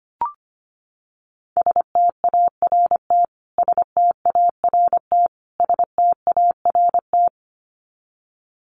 Morse code, heard as a keyed tone, about 25 wpm, 700 Hz. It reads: START START START
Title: Top 300 Words Repeated 25wpm